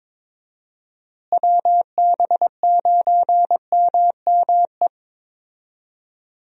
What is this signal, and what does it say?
Morse code, heard as a keyed tone, about 22 wpm, 700 Hz. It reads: WB9MME
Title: USA Call Signs - 6 Characters - Repeated 3x - 22wpm